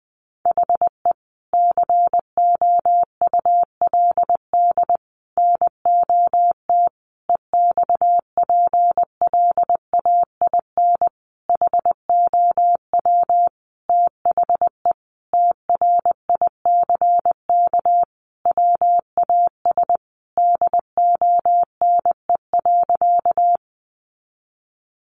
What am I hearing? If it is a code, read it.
Morse code, heard as a keyed tone, about 20 wpm, 700 Hz. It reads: HE COULD NOT EXPLAIN HOW THE TRICK WAS DONE.
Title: Sentences from Top 1500 Words 20wpm